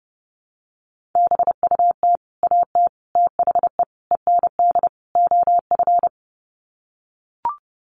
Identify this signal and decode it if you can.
Morse code, heard as a keyed tone, about 30 wpm, 700 Hz. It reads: BUT AT THE END OF
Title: Sets of 5 Words Encore2 30wpm